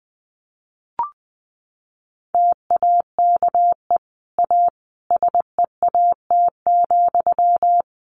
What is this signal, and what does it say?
Morse code, heard as a keyed tone, about 19 wpm, 700 Hz. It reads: TAKE A SEAT,
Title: Sentences from Top 1500 Words 20wpm